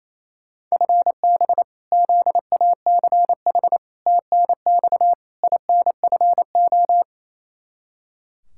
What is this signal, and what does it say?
Morse code, heard as a keyed tone, about 27 wpm, 700 Hz. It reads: FB ZACH TNX INFO
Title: K1USN SST QSO Practice 28wpm